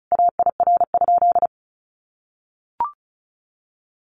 Morse code, heard as a keyed tone, about 34 wpm, 700 Hz. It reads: AIR?
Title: Sentences from Top 1200 Words 35wpm